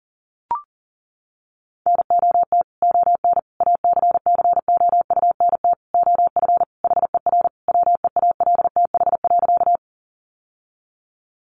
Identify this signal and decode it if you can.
Morse code, heard as a keyed tone, about 37 wpm, 700 Hz. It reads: NOT ON ACCOUNT OF HER WEALTH.
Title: Sentences from Top 1200 Words 40wpm